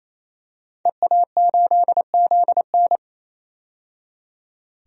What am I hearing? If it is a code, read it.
Morse code, heard as a keyed tone, about 28 wpm, 700 Hz. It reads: EA8ZN
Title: DXCC-100 Call Signs - 5 Characters - 28wpm